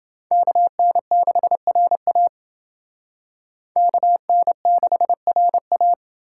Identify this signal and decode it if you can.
Morse code, heard as a keyed tone, about 29 wpm, 700 Hz. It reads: KN6RA KN6RA